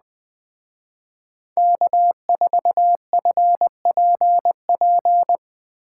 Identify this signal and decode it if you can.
Morse code, heard as a keyed tone, about 20 wpm, 700 Hz. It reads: K4FPP
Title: Speed Racing - USA Call Signs 17wpm